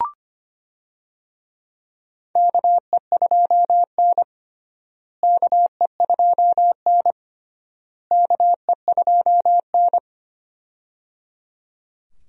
Morse code, heard as a keyed tone, about 25 wpm, 700 Hz. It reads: KE2N KE2N KE2N
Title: USA Call Signs - 4 Characters - Repeated 3x - 25wpm